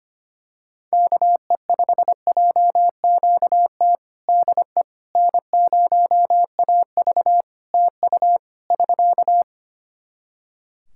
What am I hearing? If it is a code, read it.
Morse code, heard as a keyed tone, about 25 wpm, 700 Hz. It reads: KE5JQT DE N0AV TU <SK>
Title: Practice QSO - Exchange 4 - Ending - 25wpm